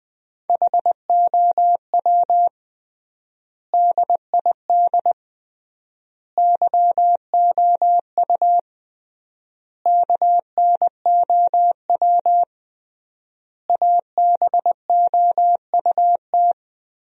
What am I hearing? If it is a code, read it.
Morse code, heard as a keyed tone, about 20 wpm, 700 Hz. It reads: HOW DID YOU KNOW ABOUT
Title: 3x Word-Spacing - Sentences from Top 100 Words - 20wpm